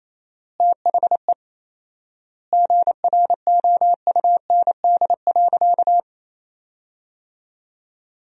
Morse code, heard as a keyed tone, about 28 wpm, 700 Hz. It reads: THE GROUND.
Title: 4x Word-Spacing - Sentences from Top 500 Words - 28wpm